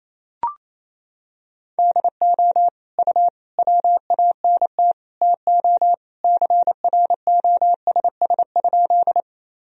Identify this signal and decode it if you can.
Morse code, heard as a keyed tone, about 28 wpm, 700 Hz. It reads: DO U WANT TO CROSS?
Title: Sentences from Top 300 Words and Abbreviations 28wpm